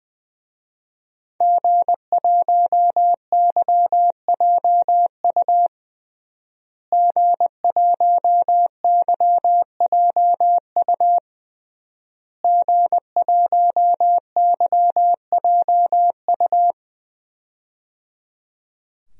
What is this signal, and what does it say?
Morse code, heard as a keyed tone, about 20 wpm, 700 Hz. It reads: G1YJU G1YJU G1YJU
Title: DXCC-100 Call Signs - 5 Characters - Repeated 3x - 20wpm